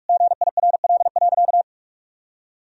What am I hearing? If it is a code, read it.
Morse code, heard as a keyed tone, about 42 wpm, 700 Hz. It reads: GIRL.
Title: Sentences from Top 400 Words and Abbreviations 45wpm